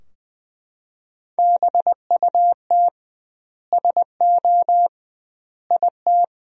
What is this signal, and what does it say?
Morse code, heard as a keyed tone, about 20 wpm, 700 Hz. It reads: BUT SO IT
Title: 2x Word-Spacing - Sentences from Top 100 Words - 20wpm